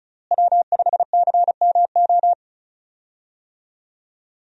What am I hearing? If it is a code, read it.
Morse code, heard as a keyed tone, about 34 wpm, 700 Hz. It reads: W5CMO
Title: USA Call Signs - 5 Characters - Repeated 2x - 35wpm